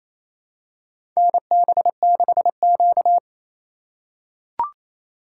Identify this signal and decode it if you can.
Morse code, heard as a keyed tone, about 28 wpm, 700 Hz. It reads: NB6Q
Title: USA Call Signs - 4 Characters - 3x Character Spacing - 28wpm